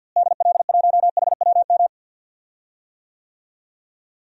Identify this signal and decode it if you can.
Morse code, heard as a keyed tone, about 50 wpm, 700 Hz. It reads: DL1HWM
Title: DXCC-100 Call Signs - 6 Characters - Repeated 3x - 50wpm